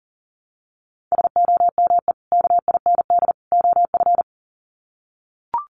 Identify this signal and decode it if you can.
Morse code, heard as a keyed tone, about 40 wpm, 700 Hz. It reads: SOME KIND OF
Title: Sets of 3 Words 40wpm